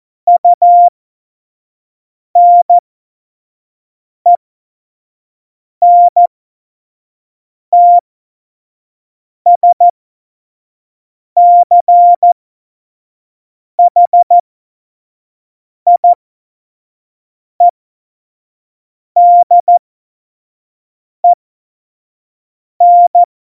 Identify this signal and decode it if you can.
Morse code, heard as a keyed tone, about 14 wpm, spaced as for 5 wpm, 700 Hz. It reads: UNENTSCHIEDEN